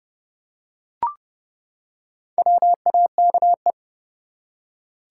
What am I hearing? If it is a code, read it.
Morse code, heard as a keyed tone, about 30 wpm, 700 Hz. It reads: WAKE
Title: Four Letter Words - 30wpm